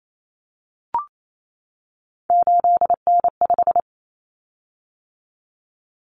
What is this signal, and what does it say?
Morse code, heard as a keyed tone, about 28 wpm, 700 Hz. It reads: 8N5
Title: Common Cut Numbers - 3-Digit 28wpm